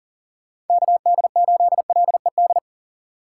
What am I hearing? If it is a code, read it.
Morse code, heard as a keyed tone, about 38 wpm, 700 Hz. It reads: KD8LED
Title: Speed Racing - USA 2x3 Call Signs 40wpm